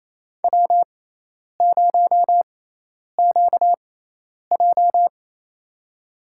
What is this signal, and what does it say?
Morse code, heard as a keyed tone, about 28 wpm, spaced as for 10 wpm, 700 Hz. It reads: W0QJ
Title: USA Call Signs - 4 Characters - 6x Character Spacing - 28wpm